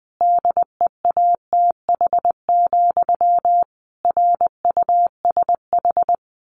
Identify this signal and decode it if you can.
Morse code, heard as a keyed tone, about 20 wpm, 700 Hz. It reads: DEATH, RUSH
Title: A Sound Of Thunder 20wpm